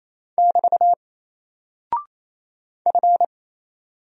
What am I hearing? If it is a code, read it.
Morse code, heard as a keyed tone, about 27 wpm, 700 Hz. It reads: <BT> F
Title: Single Character (Letter:Number:Punctuation:Prosign) 28wpm